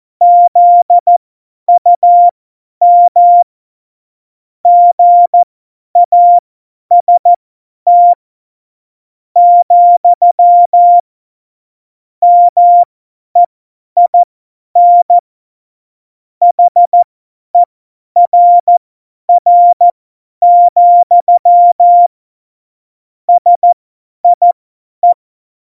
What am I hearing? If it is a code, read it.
Morse code, heard as a keyed tone, about 14 wpm, spaced as for 10 wpm, 700 Hz. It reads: ZUM GAST, MEIN HERR, SIE
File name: Morsekurs / Witze / Witz 07. Januar 2020, 50 F70